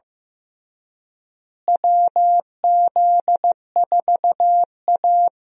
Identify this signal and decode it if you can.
Morse code, heard as a keyed tone, about 15 wpm, 700 Hz. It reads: WZ4A
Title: USA Call Signs - 4 Characters - Repeated 3x - 15wpm